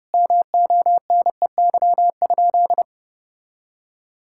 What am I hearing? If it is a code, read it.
Morse code, heard as a keyed tone, about 30 wpm, 700 Hz. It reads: MONEY?
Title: Sentences from Top 1000 Words - Review Part 5 - 30wpm